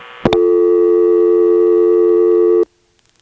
Dial tone (North America), 350 + 440 Hz.